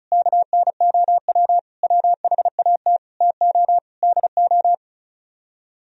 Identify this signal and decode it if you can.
Morse code, heard as a keyed tone, about 43 wpm, 700 Hz. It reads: KNOW WHAT TO DO
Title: Speed-Racing - Sets of 5 Words 35wpm